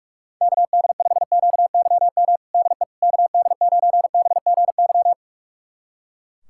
Morse code, heard as a keyed tone, about 41 wpm, 700 Hz. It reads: KD5QYM DE KD9BGY
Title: USA Callsign DE Callsign - 6 Characters - 45wpm